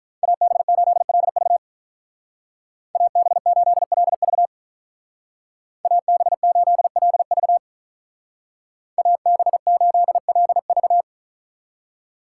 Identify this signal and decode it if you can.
Morse code, heard as a keyed tone, about 44 wpm, 700 Hz. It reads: AB8LV AB8LV AB8LV AB8LV
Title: Speed Racing - USA Call Signs 35wpm